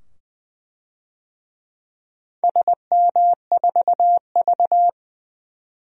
Morse code, heard as a keyed tone, about 20 wpm, 700 Hz. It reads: SM4V
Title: DXCC-100 Call Signs - 4 Characters - 20wpm